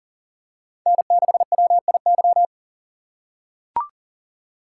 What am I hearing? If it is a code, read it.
Morse code, heard as a keyed tone, about 40 wpm, 700 Hz. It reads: N6WIY
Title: USA Call Signs - 5 Characters - 3x Character Spacing - 40wpm